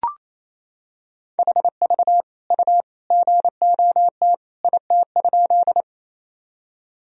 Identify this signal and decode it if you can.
Morse code, heard as a keyed tone, about 28 wpm, 700 Hz. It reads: HV U GOT IT?
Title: Sentences from Top 300 Words and Abbreviations 28wpm